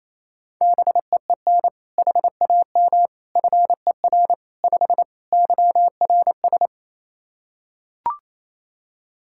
Morse code, heard as a keyed tone, about 28 wpm, 700 Hz. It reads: BEEN HAM FER 5 YRS
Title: Practice QSO - Exchange 3 - More Info - 28wpm